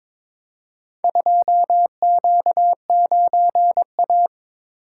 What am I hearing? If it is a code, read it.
Morse code, heard as a keyed tone, about 22 wpm, 700 Hz. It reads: 2Q9A
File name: practice / 22wpm / Four Letter-Number Combinations - 22wpm